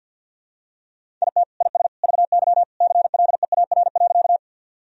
Morse code, heard as a keyed tone, about 46 wpm, 700 Hz. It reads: IT IS VY CLEAR.
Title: Sentences from Top 500 Words and Abbreviations 50wpm